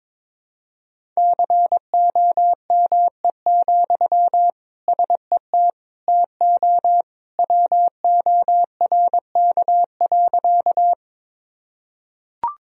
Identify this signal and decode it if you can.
Morse code, heard as a keyed tone, about 22 wpm, 700 Hz. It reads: COME, SET TO WORK.
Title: Sentences from Top 1000 Words - Review Part 1 - 22wpm